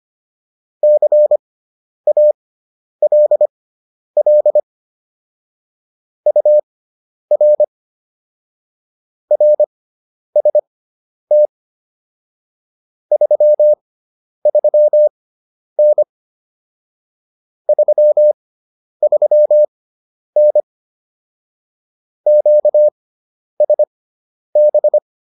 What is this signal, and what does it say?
Morse code, heard as a keyed tone, about 25 wpm, spaced as for 10 wpm, 600 Hz. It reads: CALL UR RST 33N 33N QSB